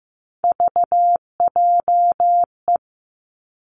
Morse code, heard as a keyed tone, about 15 wpm, 700 Hz. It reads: VJE
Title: Three Challenging Letter-Number Combinations 15wpm